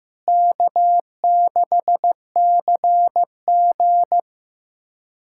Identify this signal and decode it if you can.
Morse code, heard as a keyed tone, about 15 wpm, 700 Hz. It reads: K6CG
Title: USA Callsign DE Callsign - 4 Characters - 15wpm